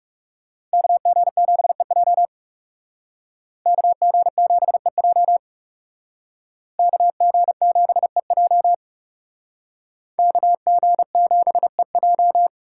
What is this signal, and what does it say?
Morse code, heard as a keyed tone, about 38 wpm, 700 Hz. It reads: KG7EJ KG7EJ KG7EJ KG7EJ